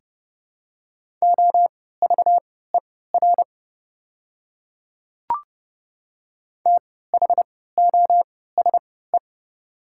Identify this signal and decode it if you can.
Morse code, heard as a keyed tone, about 30 wpm, spaced as for 17 wpm, 700 Hz. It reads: OVER THOSE
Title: 3x Character Spacing - Top 300 Words - 30wpm